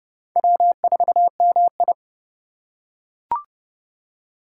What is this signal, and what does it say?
Morse code, heard as a keyed tone, about 30 wpm, 700 Hz. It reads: W4MI
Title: USA Call Signs - 2x Character Spacing - 30wpm